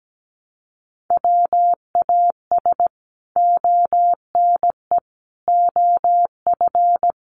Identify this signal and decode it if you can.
Morse code, heard as a keyed tone, about 17 wpm, 700 Hz. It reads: WAS ONE OF
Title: Sets of 5 Words Encore 17wpm